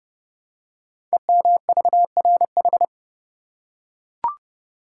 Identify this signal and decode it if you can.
Morse code, heard as a keyed tone, about 30 wpm, 700 Hz. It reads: EMVRH
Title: Code Groups - 5 Letters - 30wpm